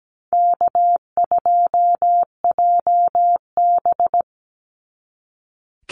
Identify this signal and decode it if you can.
Morse code, heard as a keyed tone, about 17 wpm, 700 Hz. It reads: K2JB
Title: NA SOTA Chaser Callsigns 17wpm